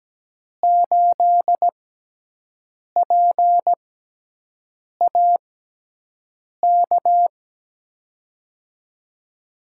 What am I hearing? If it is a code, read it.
Morse code, heard as a keyed tone, about 17 wpm, spaced as for 6 wpm, 700 Hz. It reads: 8PAK
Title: USA Call Signs - 5 Characters - 6x Character Spacing - 17wpm